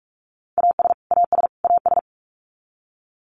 Morse code, heard as a keyed tone, about 44 wpm, 700 Hz. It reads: AS AS AS
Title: Top 300 Words Repeated 45wpm